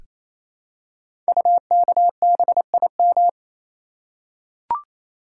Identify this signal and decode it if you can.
Morse code, heard as a keyed tone, about 27 wpm, 700 Hz. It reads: UKBIM
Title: Code Groups - 5 Letters - 28wpm